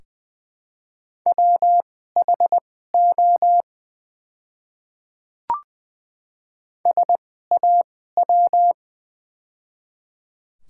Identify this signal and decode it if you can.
Morse code, heard as a keyed tone, about 20 wpm, spaced as for 15 wpm, 690 Hz. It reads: WHO SAW
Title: 2x Character Spacing - Top 500 Words - 20wpm